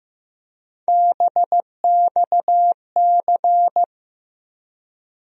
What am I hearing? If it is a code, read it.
Morse code, heard as a keyed tone, about 15 wpm, 700 Hz. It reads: BXC
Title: USA Call Sign Postfixes - 3 Letters 15wpm